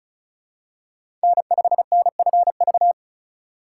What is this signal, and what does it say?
Morse code, heard as a keyed tone, about 36 wpm, 700 Hz. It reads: N5NFV
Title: USA Call Signs - 5 Characters - 3x Character Spacing - 35wpm